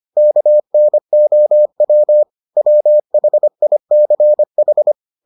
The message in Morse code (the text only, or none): KNOW WHICH